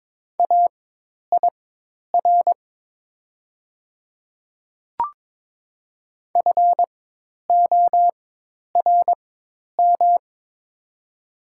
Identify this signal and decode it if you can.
Morse code, heard as a keyed tone, about 23 wpm, spaced as for 10 wpm, 700 Hz. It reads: AIR FORM
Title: 4x Character Spacing - Top 500 Words - 22wpm